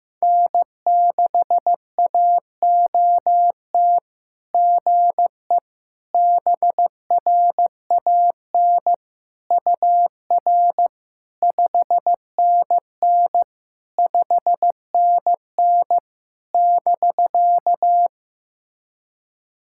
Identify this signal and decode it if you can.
Morse code, heard as a keyed tone, about 16 wpm, 690 Hz. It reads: N6AOT GE BRAN UR 5NN 5NN <BK>
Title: SOTA QSO Practice 15wpm